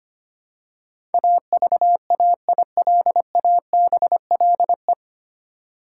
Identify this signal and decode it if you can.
Morse code, heard as a keyed tone, about 25 wpm, 690 Hz. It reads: AVAILABLE
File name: practice / 25wpm / 9 Letter Words - First 3 Letters Given - 25wpm